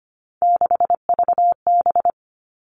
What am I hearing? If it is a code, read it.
Morse code, heard as a keyed tone, about 25 wpm, 700 Hz. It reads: 6VB
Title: All Cut Numbers - 3-Digit 25wpm